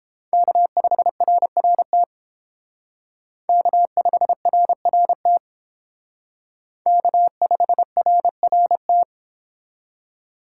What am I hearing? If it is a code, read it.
Morse code, heard as a keyed tone, about 29 wpm, 700 Hz. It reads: K5RRT K5RRT K5RRT